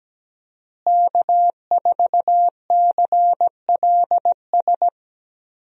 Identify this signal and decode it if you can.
Morse code, heard as a keyed tone, about 17 wpm, 700 Hz. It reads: K4CLS
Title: USA Call Signs - 5 Characters - Repeated 2x - 17wpm